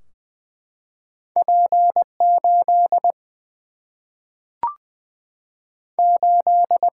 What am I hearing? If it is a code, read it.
Morse code, heard as a keyed tone, about 21 wpm, 700 Hz. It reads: P 8 8
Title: Single-Letter Single-Number 20wpm